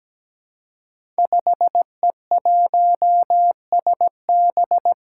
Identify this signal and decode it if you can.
Morse code, heard as a keyed tone, about 17 wpm, 700 Hz. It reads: 5E1SB
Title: Five Challenging Letter-Number Combinations 17wpm